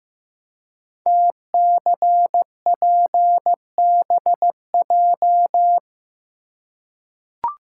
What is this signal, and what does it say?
Morse code, heard as a keyed tone, about 15 wpm, 700 Hz. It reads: TCPBJ
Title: Code Groups - 5 Letters - 15wpm